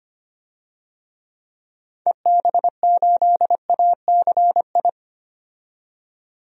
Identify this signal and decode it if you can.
Morse code, heard as a keyed tone, about 25 wpm, 700 Hz. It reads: EB8ACI
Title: DXCC-100 Call Signs - 6 Characters - 25wpm